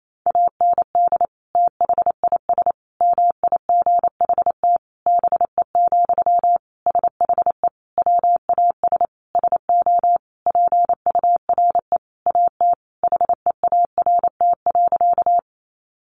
Morse code, heard as a keyed tone, about 28 wpm, 700 Hz. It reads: AND THIS MIGHT BE, SHE WAS SO PURE AT HEART.
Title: Sentences from Top 1500 Words 28wpm